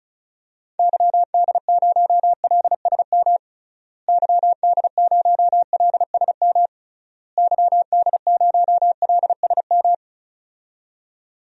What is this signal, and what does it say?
Morse code, heard as a keyed tone, about 36 wpm, 700 Hz. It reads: YD0LSM YD0LSM YD0LSM
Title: DXCC-100 Call Signs - 6 Characters - Repeated 3x - 35wpm